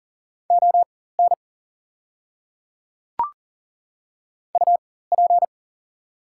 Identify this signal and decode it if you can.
Morse code, heard as a keyed tone, about 40 wpm, spaced as for 19 wpm, 690 Hz. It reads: ON UP
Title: 4x Character Spacing - Top 100 Words - 40wpm